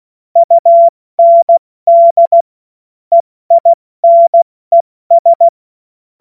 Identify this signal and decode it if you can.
Morse code, heard as a keyed tone, about 16 wpm, spaced as for 14 wpm, 680 Hz. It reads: UND EINES
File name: Morsekurs / Witze / Witz 27 Mai 2025 70 F80